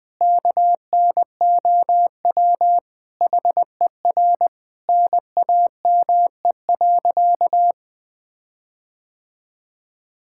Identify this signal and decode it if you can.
Morse code, heard as a keyed tone, about 20 wpm, 700 Hz. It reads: KNOW HER NAME.
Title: Sentences with 5-Word Phrases - 20wpm